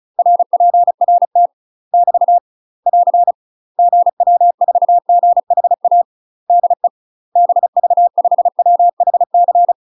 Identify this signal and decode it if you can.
Morse code, heard as a keyed tone, about 34 wpm, 700 Hz. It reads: RPRT <BT> <AR> GW4GHA DE BV5WHC
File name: UK MASTER QSOs TRAINING / 35WPM / UK MASTER QSOs 0015